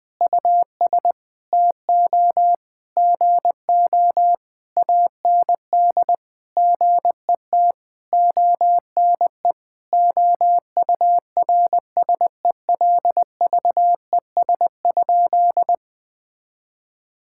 Morse code, heard as a keyed tone, about 21 wpm, 700 Hz. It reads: US TO GO AND GET ONE OURSELVES?
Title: Sentences from Top 1600 Words 20wpm